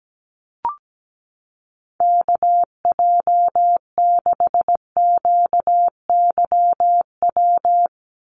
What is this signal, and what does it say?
Morse code, heard as a keyed tone, about 17 wpm, 700 Hz. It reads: KJ6QYW
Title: USA Call Signs 17wpm